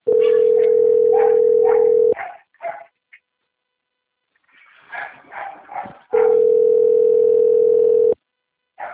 Ringback tone (North America), 440 + 480 Hz.
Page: Guarapuava - PR